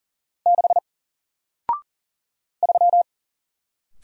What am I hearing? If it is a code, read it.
Morse code, heard as a keyed tone, about 40 wpm, 700 Hz. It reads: 6 3
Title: Single Letter-Number Alt 40wpm